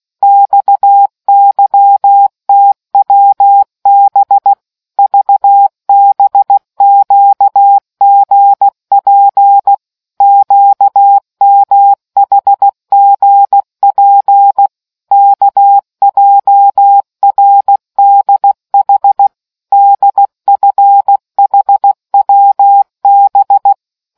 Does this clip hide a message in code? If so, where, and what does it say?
Yes: Morse code from 0.2 to 23.7 s, heard as a keyed tone, about 16 wpm, 800 Hz. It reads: XYTWB VBQGP QMHGP KJRDH DFHWB